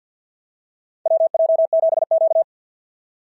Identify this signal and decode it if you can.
Morse code, heard as a keyed tone, about 52 wpm, 690 Hz. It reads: WJ7Q
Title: SOTA QSO Practice 50wpm